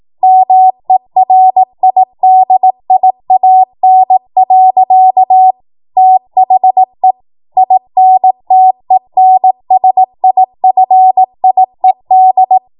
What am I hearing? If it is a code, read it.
Morse code, heard as a keyed tone, about 18 wpm, 750 Hz. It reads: MERIDIAN. THE INTENSIFIED
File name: Morse / 18WPM